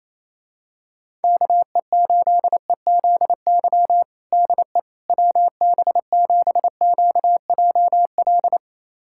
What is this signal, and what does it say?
Morse code, heard as a keyed tone, about 28 wpm, 700 Hz. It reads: KE8EZY DE WB7QJL
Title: USA Callsign DE Callsign - 6 Characters - 28wpm